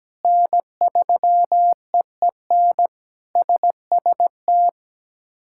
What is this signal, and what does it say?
Morse code, heard as a keyed tone, about 17 wpm, 700 Hz. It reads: N3EEN SST
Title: K1USN SST QSO Practice - 17wpm